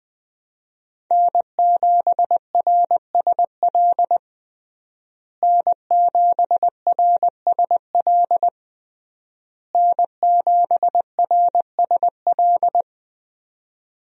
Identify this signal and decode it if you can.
Morse code, heard as a keyed tone, about 20 wpm, 700 Hz. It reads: N7RSL N7RSL N7RSL
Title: USA Call Signs - 5 Characters - Repeated 3x - 20wpm